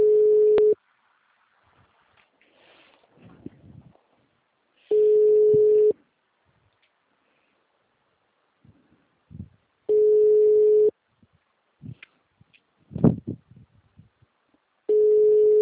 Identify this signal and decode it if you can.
Ringback tone (Europe, 425 Hz).
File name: out- -2007- - - .18096